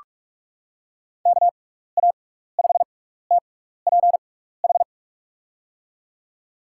Morse code, heard as a keyed tone, about 44 wpm, spaced as for 15 wpm, 700 Hz. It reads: KA5TPH
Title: USA Call Signs - 6 Characters - 6x Character Spacing - 45wpm